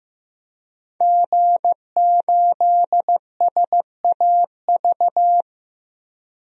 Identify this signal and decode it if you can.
Morse code, heard as a keyed tone, about 15 wpm, 700 Hz. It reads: G8SAV
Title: DXCC-100 Call Signs - 5 Characters - Repeated 2x - 15wpm